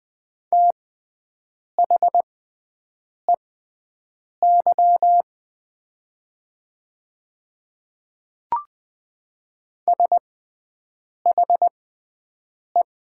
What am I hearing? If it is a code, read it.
Morse code, heard as a keyed tone, about 20 wpm, spaced as for 7 wpm, 700 Hz. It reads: THEY SHE
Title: 6x Character Spacing - Top 500 Words - 20wpm